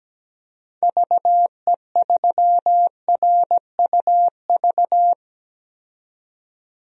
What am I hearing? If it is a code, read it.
Morse code, heard as a keyed tone, about 17 wpm, 700 Hz. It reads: VE3RUV
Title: DXCC-100 Call Signs - 6 Characters - 17wpm